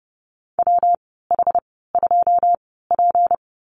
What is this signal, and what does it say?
Morse code, heard as a keyed tone, about 30 wpm, spaced as for 17 wpm, 700 Hz. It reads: WH2P
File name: Lesson 37g - Letter x - Call Signs 30wpm